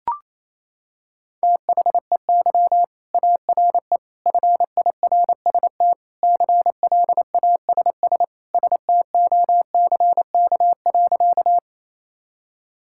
Morse code, heard as a keyed tone, about 27 wpm, 700 Hz. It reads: THEY ARE FIRST CLASS STOCK.
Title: Sentences from Top 1000 Words - Review Part 5 - 28wpm